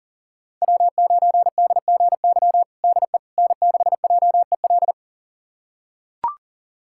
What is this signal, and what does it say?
Morse code, heard as a keyed tone, about 40 wpm, 690 Hz. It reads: W9DGY DE N6JEL
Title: USA Callsign DE Callsign - 5 Characters - 40wpm